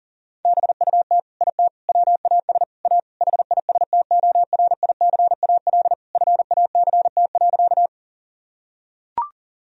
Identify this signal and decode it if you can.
Morse code, heard as a keyed tone, about 40 wpm, 700 Hz. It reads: BUT IT WAS A HISTORICAL FACT.
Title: Sentences from Top 1000 Words - Review Part 4 - 40wpm